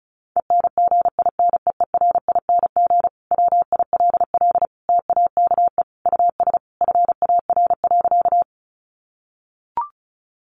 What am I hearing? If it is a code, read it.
Morse code, heard as a keyed tone, about 34 wpm, 700 Hz. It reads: ENGINEERING WILL TAKE US FAR.
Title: Sentences from Top 1500 Words 35wpm